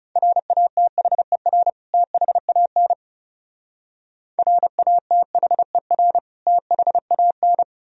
Morse code, heard as a keyed tone, about 37 wpm, 700 Hz. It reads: RATHER THAN RATHER THAN
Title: Speed-Racing - Sets of 2 Words 30wpm